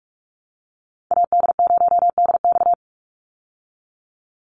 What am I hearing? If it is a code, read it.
Morse code, heard as a keyed tone, about 44 wpm, 700 Hz. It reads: AD0DX
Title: Call Signs Repeated 2x - 45wpm